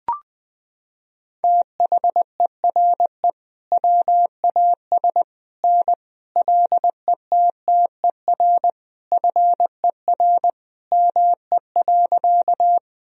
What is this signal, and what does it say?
Morse code, heard as a keyed tone, about 21 wpm, 700 Hz. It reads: THERE WAS N LETTER FER ME.
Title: Sentences from Top 300 Words and Abbreviations 20wpm